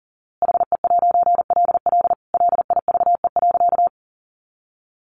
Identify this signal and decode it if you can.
Morse code, heard as a keyed tone, about 40 wpm, 700 Hz. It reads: HE'LL LIVE.
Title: A Sound Of Thunder 40wpm